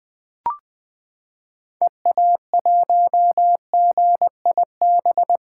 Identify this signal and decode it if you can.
Morse code, heard as a keyed tone, about 20 wpm, 700 Hz. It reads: EA1GIB
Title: 2023 POTA Hunter Call Signs - 20wpm